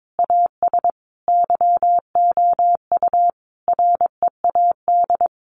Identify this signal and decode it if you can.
Morse code, heard as a keyed tone, about 22 wpm, 700 Hz. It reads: AS YOU READ
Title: Sentences from Top 1300 Words 22wpm